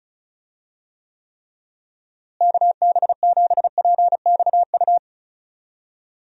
Morse code, heard as a keyed tone, about 34 wpm, 700 Hz. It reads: KB7PXU
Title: USA Call Signs - 6 Characters - Repeated 3x - 35wpm